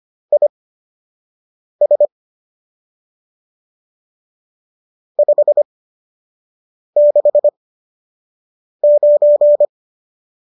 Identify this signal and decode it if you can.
Morse code, heard as a keyed tone, about 25 wpm, spaced as for 6 wpm, 590 Hz. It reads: IS 569